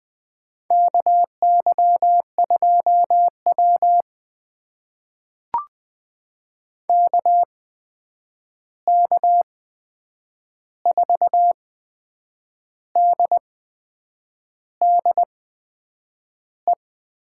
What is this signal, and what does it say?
Morse code, heard as a keyed tone, about 20 wpm, spaced as for 8 wpm, 700 Hz. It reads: KY2W KK4DDE
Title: USA Call Signs - 8x Character Spacing - 20wpm